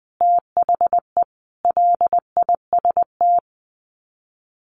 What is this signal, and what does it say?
Morse code, heard as a keyed tone, about 20 wpm, 700 Hz. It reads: THE LIST
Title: Sets of 5 Words Encore2 20wpm